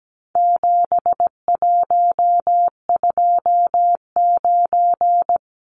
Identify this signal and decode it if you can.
Morse code, heard as a keyed tone, about 17 wpm, 700 Hz. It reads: 7129
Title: Numbers - 4-Digit 17wpm